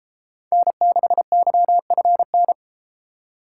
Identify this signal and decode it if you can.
Morse code, heard as a keyed tone, about 30 wpm, 700 Hz. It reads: N6YFN